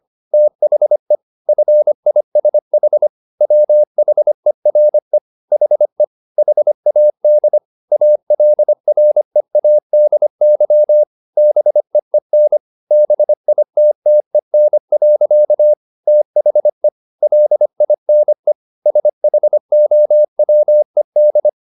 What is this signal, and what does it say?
Morse code, heard as a keyed tone, about 25 wpm, 600 Hz. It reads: THE FISH WHERE HE HAD ALREADY BEEN BITTEN. THE LINE SHOWED